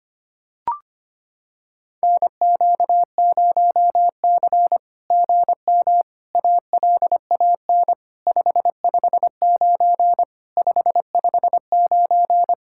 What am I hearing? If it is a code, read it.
Morse code, heard as a keyed tone, about 25 wpm, 690 Hz. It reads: NQ0C GM ALAN 559 559
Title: SOTA QSO Practice 25wpm